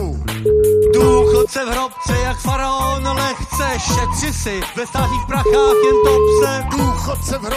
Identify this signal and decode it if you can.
Ringback tone (Europe, 425 Hz), with speech.